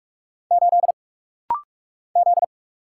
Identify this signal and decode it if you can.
Morse code, heard as a keyed tone, about 44 wpm, 700 Hz. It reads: 8 Z
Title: Single Letter-Number Alt - Rapid-Fire 45wpm